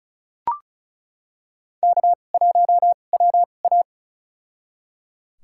Morse code, heard as a keyed tone, about 36 wpm, spaced as for 25 wpm, 700 Hz. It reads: K1WA
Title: USA Call Signs - 4 Characters - 2x Character Spacing - 35wpm